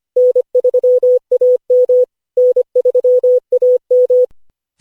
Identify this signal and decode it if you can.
Morse code, heard as a keyed tone, about 25 wpm, 500 Hz. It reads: N3AM N3AM
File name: cw / Session5 cw training